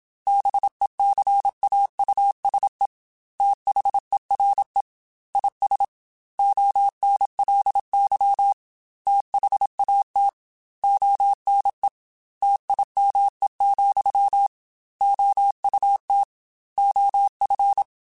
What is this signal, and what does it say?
Morse code, heard as a keyed tone, about 27 wpm, 800 Hz. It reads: BECAUSE THERE IS ONLY THAT ONE TIME, OUT OF
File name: Technical Subjects / Technical Subjects-mp3s / AND Gate @25